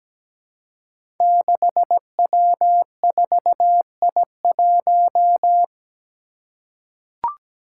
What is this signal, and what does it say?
Morse code, heard as a keyed tone, about 17 wpm, 700 Hz. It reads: 6W4I1
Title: Five Challenging Letter-Number Combinations 17wpm